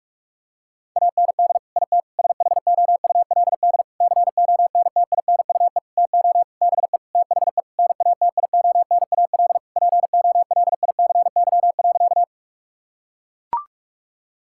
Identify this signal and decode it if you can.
Morse code, heard as a keyed tone, about 41 wpm, 700 Hz. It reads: AND IT SHOULD CONTINUE TO BE THE NATIONAL POLICY.
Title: Sentences from Top 1000 Words - Review Part 6 - 45wpm